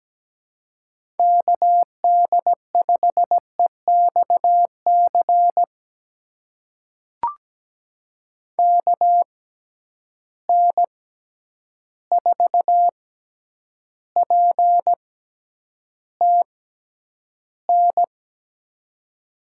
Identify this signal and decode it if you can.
Morse code, heard as a keyed tone, about 17 wpm, spaced as for 9 wpm, 700 Hz. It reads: KD5EXC KN4PTN
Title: USA Call Signs - 6 Characters - 6x Character Spacing - 17wpm